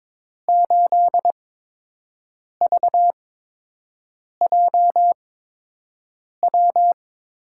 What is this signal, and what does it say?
Morse code, heard as a keyed tone, about 22 wpm, spaced as for 6 wpm, 700 Hz. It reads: 8VJW
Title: USA Call Signs - 6 Characters - 8x Character Spacing - 22wpm